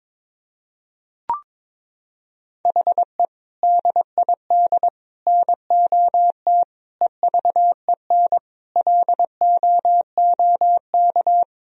Morse code, heard as a keyed tone, about 22 wpm, 700 Hz. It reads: HE DID NOT EVEN LOOK
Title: Sentences from Top 1000 Words - Review Part 1 - 22wpm